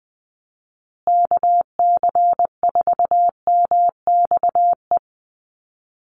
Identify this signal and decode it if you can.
Morse code, heard as a keyed tone, about 20 wpm, 700 Hz. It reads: KC4MXE
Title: USA Call Signs Repeated 3x 20wpm